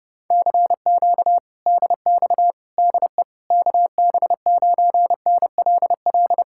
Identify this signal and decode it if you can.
Morse code, heard as a keyed tone, about 32 wpm, 700 Hz. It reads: CQ DX DE KB9NLL